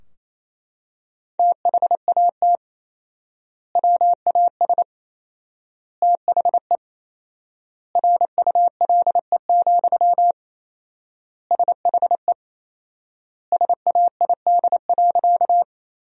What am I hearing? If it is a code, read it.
Morse code, heard as a keyed tone, about 27 wpm, 700 Hz. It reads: THAT WAS THE RULE, SHE SAID.
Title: 4x Word-Spacing - Sentences from Top 400 Words - 28wpm